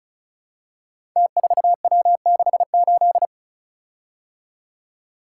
Morse code, heard as a keyed tone, about 34 wpm, 700 Hz. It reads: T4W68
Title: Code Groups - 5 Letter:Numbers - 35wpm